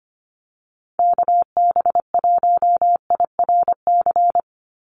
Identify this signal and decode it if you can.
Morse code, heard as a keyed tone, about 25 wpm, 700 Hz. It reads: KB1IRC
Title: USA Call Signs 25wpm